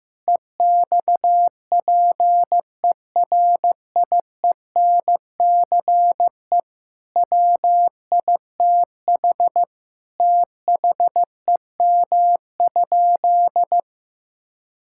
Morse code, heard as a keyed tone, about 16 wpm, 700 Hz. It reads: EXPERIENCE WITH THEM?
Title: Sentences from Top 1000 Words - Review Part 5 - 15wpm